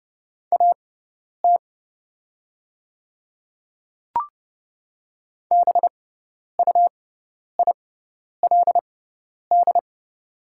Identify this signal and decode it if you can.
Morse code, heard as a keyed tone, about 30 wpm, spaced as for 10 wpm, 700 Hz. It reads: AT BUILD
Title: 6x Character Spacing - Top 300 Words - 30wpm